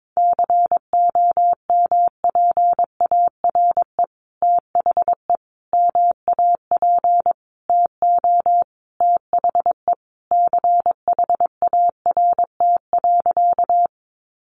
Morse code, heard as a keyed tone, about 22 wpm, 700 Hz. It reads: COMPARE THE MAP TO THE CHART.
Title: Sentences from Top 1300 Words 22wpm